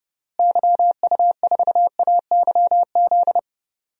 Morse code, heard as a keyed tone, about 30 wpm, 700 Hz. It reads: YU4AYZ
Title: DXCC-100 Call Signs - 6 Characters - Repeated 3x - 30wpm